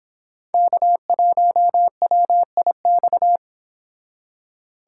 Morse code, heard as a keyed tone, about 25 wpm, 700 Hz. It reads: K1WIX